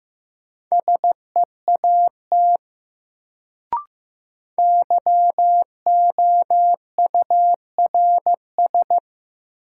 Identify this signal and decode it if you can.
Morse code, heard as a keyed tone, about 15 wpm, 700 Hz. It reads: SEAT YOURS
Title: Sentences from Top 1500 Words - New Word Drill Alt - 15wpm